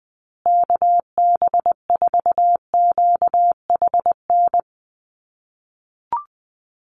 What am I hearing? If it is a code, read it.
Morse code, heard as a keyed tone, about 20 wpm, 700 Hz. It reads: KB4QHN
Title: USA Call Signs Repeated 3x 20wpm